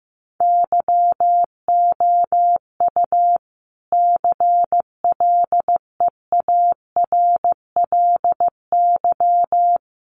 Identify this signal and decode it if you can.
Morse code, heard as a keyed tone, about 15 wpm, 700 Hz. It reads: YOU CLEARLY
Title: Sentences from Top 1300 Words 15wpm